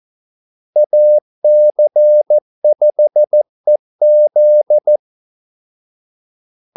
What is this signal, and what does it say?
Morse code, heard as a keyed tone, about 14 wpm, 600 Hz. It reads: AC5EZ
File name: Callsigns-14wpm